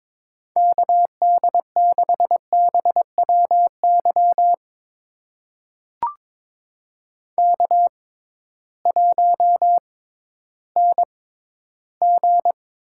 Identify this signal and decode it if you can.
Morse code, heard as a keyed tone, about 22 wpm, spaced as for 13 wpm, 700 Hz. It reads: KD6BWY K1NG
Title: USA Call Signs - 6x Character Spacing - 22wpm